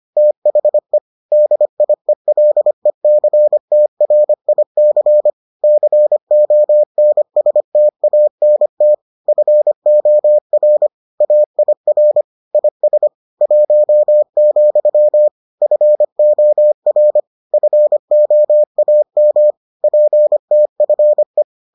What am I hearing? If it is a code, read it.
Morse code, heard as a keyed tone, about 25 wpm, 600 Hz. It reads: THE DIELECTRIC CONSTANT FOR AIR IS 1, FOR FOAM PTFE